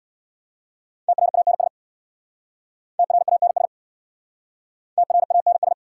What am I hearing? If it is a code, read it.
Morse code, heard as a keyed tone, about 66 wpm, 690 Hz. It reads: TRANS TRANS TRANS